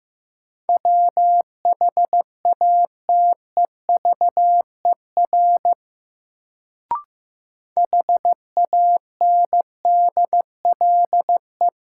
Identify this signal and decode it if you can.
Morse code, heard as a keyed tone, about 15 wpm, 700 Hz. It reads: WHATEVER HANDLE
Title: Sentences from Top 1100 Words - New Word Drill - 15wpm